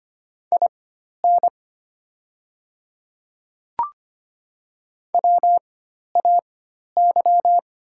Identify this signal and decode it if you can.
Morse code, heard as a keyed tone, about 25 wpm, spaced as for 12 wpm, 700 Hz. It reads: IN WAY
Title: 4x Character Spacing - Top 100 Words - 25wpm